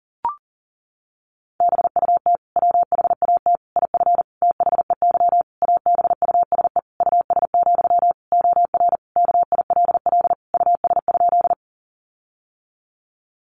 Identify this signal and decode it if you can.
Morse code, heard as a keyed tone, about 37 wpm, 700 Hz. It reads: BUT WHAT IF THEY ABUSE US, OR KILL US?
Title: Sentences from Top 1400 Words 40wpm